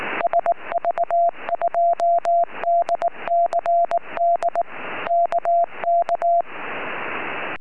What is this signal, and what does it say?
Morse code, heard as a keyed tone, about 19 wpm, 690 Hz. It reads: SV2DCD KK
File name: sv2dcd 4m